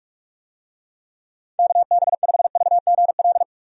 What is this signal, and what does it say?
Morse code, heard as a keyed tone, about 46 wpm, 700 Hz. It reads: KB5VGL
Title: USA Call Signs - 6 Characters - Repeated 3x - 45wpm